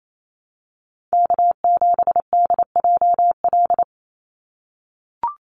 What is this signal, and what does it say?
Morse code, heard as a keyed tone, about 28 wpm, 700 Hz. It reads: K7DJL
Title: NA SOTA Chaser Callsigns 28wpm